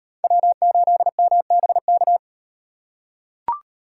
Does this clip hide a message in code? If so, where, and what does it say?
Yes: Morse code from 0.2 to 2.2 s, heard as a keyed tone, about 36 wpm, 700 Hz. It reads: W8MBK